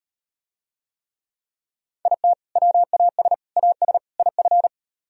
Morse code, heard as a keyed tone, about 38 wpm, 700 Hz. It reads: IT WAS AS IF